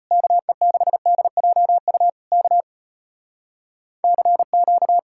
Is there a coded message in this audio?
Yes: Morse code from 0.1 to 5.0 s, heard as a keyed tone, about 33 wpm, 700 Hz. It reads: KE6DJU K CQ